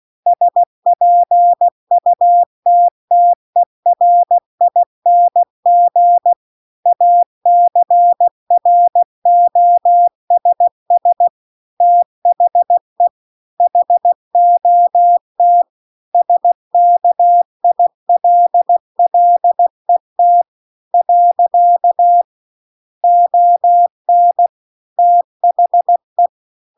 Morse code, heard as a keyed tone, about 16 wpm, 700 Hz. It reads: SPUTTERING ACROSS THE HOT SKILLET . ON THE
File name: CW-The-Big-Twoheaded-River-2-16WPM